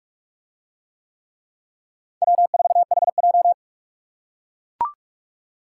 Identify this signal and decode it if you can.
Morse code, heard as a keyed tone, about 43 wpm, 700 Hz. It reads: W4HJ